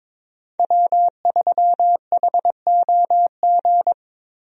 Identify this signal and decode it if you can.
Morse code, heard as a keyed tone, about 22 wpm, 690 Hz. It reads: W3HOG
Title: USA Call Signs - 5 Characters - Repeated 3x - 22wpm